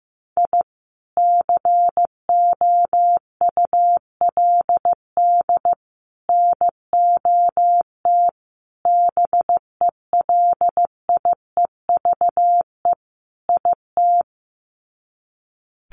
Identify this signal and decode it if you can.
Morse code, heard as a keyed tone, about 15 wpm, 700 Hz. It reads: I COULD NOT BELIEVE IT
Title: Sets of 5 Words Encore 15wpm